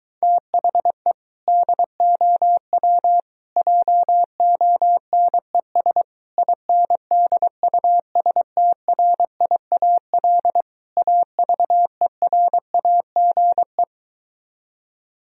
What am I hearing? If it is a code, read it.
Morse code, heard as a keyed tone, about 23 wpm, 700 Hz. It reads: THE DOW JONES INDUSTRIAL AVERAGE